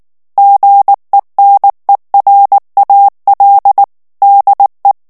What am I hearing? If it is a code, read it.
Morse code, heard as a keyed tone, about 19 wpm, 800 Hz. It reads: GENERAL DE
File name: ESPAGNOL TEXTE EN CLAIR 2 VITESSE 18 MOTS MINUTES